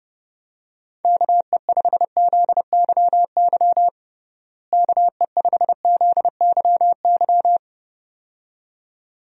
Morse code, heard as a keyed tone, about 30 wpm, 700 Hz. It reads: KE5ZYY KE5ZYY
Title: USA Call Signs - 6 Characters - Repeated 2x - 30wpm